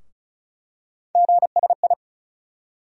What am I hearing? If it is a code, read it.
Morse code, heard as a keyed tone, about 36 wpm, 700 Hz. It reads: GSI
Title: USA Call Sign Postfixes - 3 Letters 35wpm